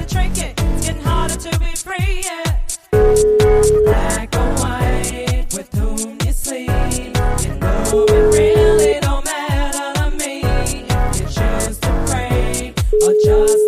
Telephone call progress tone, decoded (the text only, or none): ringback tone